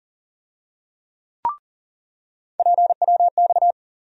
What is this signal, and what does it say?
Morse code, heard as a keyed tone, about 40 wpm, 700 Hz. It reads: PWX
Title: USA Call Sign Postfixes - 3 Letters 40wpm